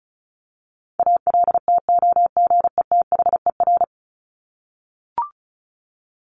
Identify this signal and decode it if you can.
Morse code, heard as a keyed tone, about 34 wpm, 700 Hz. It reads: ALTOGETHER
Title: Top 1501-1750 Words 35wpm